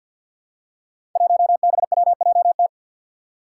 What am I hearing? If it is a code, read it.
Morse code, heard as a keyed tone, about 52 wpm, 700 Hz. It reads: 1BWJT
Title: Five Challenging Letter-Number Combinations 50wpm